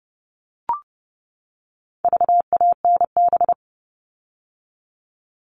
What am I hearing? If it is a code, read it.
Morse code, heard as a keyed tone, about 30 wpm, 700 Hz. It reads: VANB
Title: All Cut Numbers - 4-Digit 30wpm